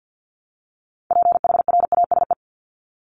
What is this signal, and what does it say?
Morse code, heard as a keyed tone, about 48 wpm, 700 Hz. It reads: PHRASE
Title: Top 1501-1750 Words 50wpm